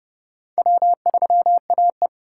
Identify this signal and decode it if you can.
Morse code, heard as a keyed tone, about 28 wpm, 700 Hz. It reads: W3AE